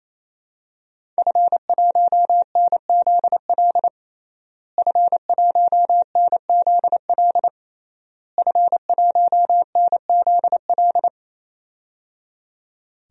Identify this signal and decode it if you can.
Morse code, heard as a keyed tone, about 27 wpm, 700 Hz. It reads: F1NZL F1NZL F1NZL
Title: DXCC-100 Call Signs - 5 Characters - Repeated 3x - 28wpm